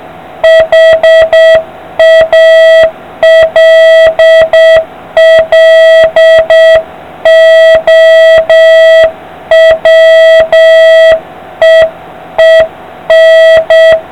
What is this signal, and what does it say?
Morse code, heard as a keyed tone, about 7 wpm, 650 Hz, with noise. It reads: HALLOWEEN